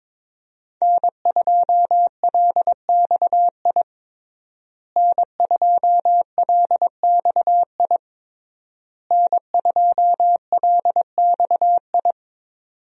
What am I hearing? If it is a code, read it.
Morse code, heard as a keyed tone, about 22 wpm, 700 Hz. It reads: N2LXI N2LXI N2LXI
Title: USA Call Signs - 5 Characters - Repeated 3x - 22wpm